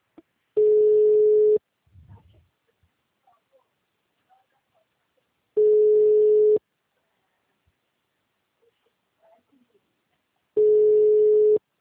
Ringback tone (Europe, 425 Hz).